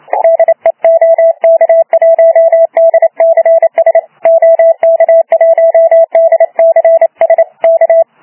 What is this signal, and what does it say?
Morse code, heard as a keyed tone, about 29 wpm, 660 Hz. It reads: DE OK1DCS OK1DCS K
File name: Archiv / CUC318 / CUC318 záznam OK1DCS